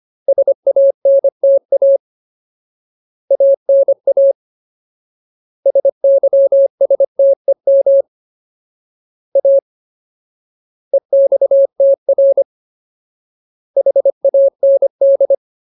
Morse code, heard as a keyed tone, about 25 wpm, 550 Hz. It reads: SANTA ANA SYSTEM A EXTR HAND
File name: Ultimate Word List / 25WPM Ultimate